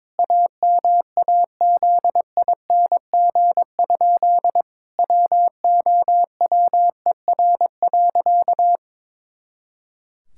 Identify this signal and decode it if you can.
Morse code, heard as a keyed tone, about 22 wpm, 700 Hz. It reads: AMAZING? WOWER.
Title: Pangram Challenge v2 - 22wpm